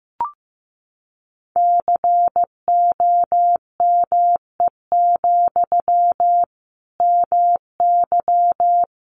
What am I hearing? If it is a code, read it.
Morse code, heard as a keyed tone, about 15 wpm, 700 Hz. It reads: COME, MY
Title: Sentences from Top 800 Words 15wpm